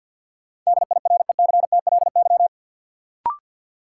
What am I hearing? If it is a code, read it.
Morse code, heard as a keyed tone, about 53 wpm, 700 Hz. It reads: DIRECTLY
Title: Sentences from Top 800 Words - New Word Drill - 50wpm